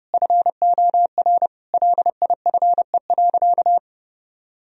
Morse code, heard as a keyed tone, about 28 wpm, 700 Hz. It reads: FOR LIFE.
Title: Sentences from Top 1000 Words - Review Part 4 - 30wpm